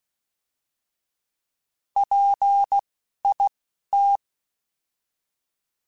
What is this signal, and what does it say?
Morse code, heard as a keyed tone, about 16 wpm, spaced as for 11 wpm, 800 Hz. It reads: PIT
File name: Alphabetical / Alphabetical mp3s / Ps @10